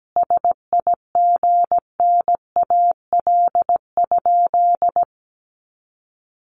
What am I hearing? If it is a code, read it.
Morse code, heard as a keyed tone, about 17 wpm, 700 Hz. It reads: SIGNAL?
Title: Sentences from Top 1400 Words 17wpm